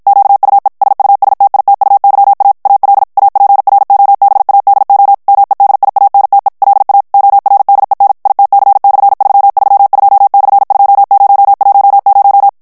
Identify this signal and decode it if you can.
Morse code, heard as a keyed tone, about 54 wpm, 800 Hz. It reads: QUE SUSTITUYA AL APROBADO MEDIANTE LA ORDEN ITC/332/2010
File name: morse mp3 / ESPAGNOL TEXTE EN CLAIR 2 VITESSE 50 MOTS MINUTES